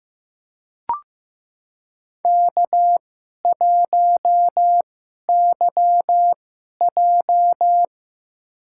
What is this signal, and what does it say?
Morse code, heard as a keyed tone, about 15 wpm, spaced as for 11 wpm, 700 Hz. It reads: K1YJ
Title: USA Call Signs - 2x Character Spacing - 15wpm